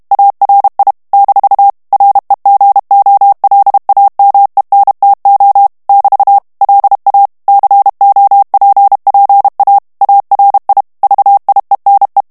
Morse code, heard as a keyed tone, about 32 wpm, 800 Hz. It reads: ARI - REGOLAMENTO <BT> LA COPPA ARI VIENE
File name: texte clair 2 vitesse 30 mots minute